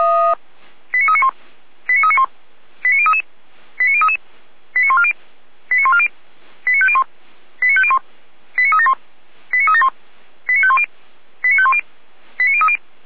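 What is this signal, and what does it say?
Five-tone selective call (ZVEI), heard as quick runs of five tones, tones of about 70 ms. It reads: 89391 89391 89033 89033 89140 89140 89501 89501 89371 89371 89510 89510 89030
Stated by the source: ZVEI five-tone selective call received on SDR, from the feuerwehr-orlamuende.org fire brigade server.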